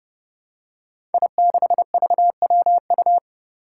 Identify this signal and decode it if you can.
Morse code, heard as a keyed tone, about 31 wpm, 700 Hz. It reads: I6VWU
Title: Five Challenging Letter-Number Combinations 30wpm